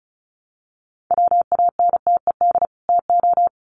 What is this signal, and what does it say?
Morse code, heard as a keyed tone, about 34 wpm, 700 Hz. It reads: WANTED TO
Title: Sets of 2 Words 35wpm